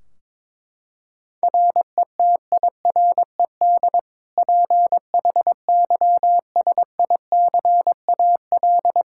Morse code, heard as a keyed tone, about 21 wpm, 700 Hz. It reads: RETIRED PHYSICAL
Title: Practice QSO - Exchange 3 - More Info - 22wpm